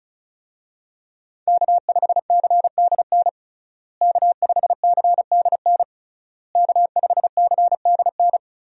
Morse code, heard as a keyed tone, about 34 wpm, 700 Hz. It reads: K5CDN K5CDN K5CDN
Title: USA Call Signs - 5 Characters - Repeated 3x - 35wpm